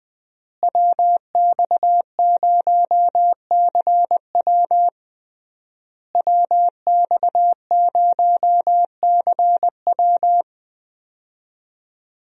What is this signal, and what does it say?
Morse code, heard as a keyed tone, about 20 wpm, 700 Hz. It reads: WX0CW WX0CW
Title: USA Call Signs - 5 Characters - Repeated 3x - 20wpm